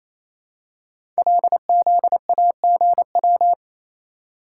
Morse code, heard as a keyed tone, about 27 wpm, 700 Hz. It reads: LZAGW
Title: Code Groups - 5 Letter:Numbers - 28wpm